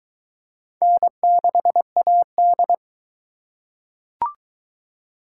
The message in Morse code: N6AD

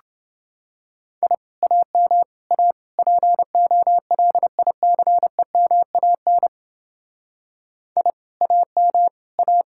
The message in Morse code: I AM A POLICEMAN I AM A

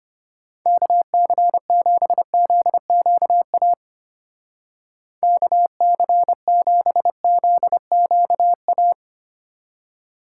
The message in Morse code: KC7ZQA KC7ZQA

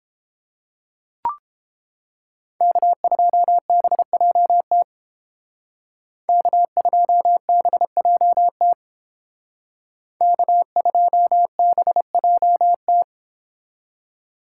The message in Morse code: K2BJT K2BJT K2BJT

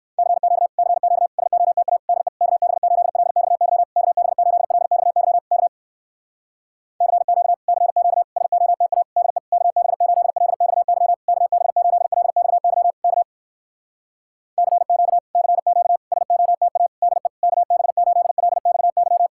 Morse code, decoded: CQ CQ SOTA DE KB8LXY KB8LXY K CQ CQ SOTA DE KB8LXY KB8LXY K CQ CQ SOTA DE KB8LXY

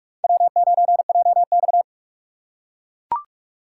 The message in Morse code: W9JX

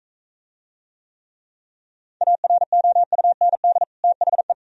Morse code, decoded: AROUND THE

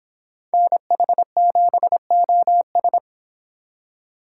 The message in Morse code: NH7OS

